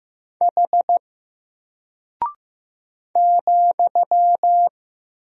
Morse code, heard as a keyed tone, about 15 wpm, 700 Hz. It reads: H ,
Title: Single Character (Letter:Number:Punctuation:Prosign) 15wpm